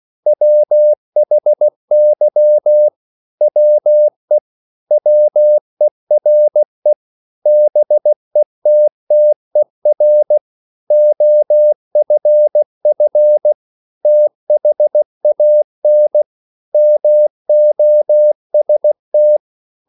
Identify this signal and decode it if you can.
Morse code, heard as a keyed tone, about 16 wpm, 600 Hz. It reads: WHY WE WERE BETTER OFF THAN MOST